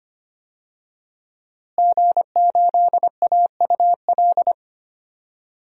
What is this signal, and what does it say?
Morse code, heard as a keyed tone, about 25 wpm, 700 Hz. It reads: G8AUL
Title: DXCC-100 Call Signs - 5 Characters - Repeated 2x - 25wpm